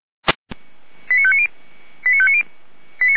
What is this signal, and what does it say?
Five-tone selective call (ZVEI), heard as quick runs of five tones, tones of about 70 ms. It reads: 89590 89590
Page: SDR/ZVEI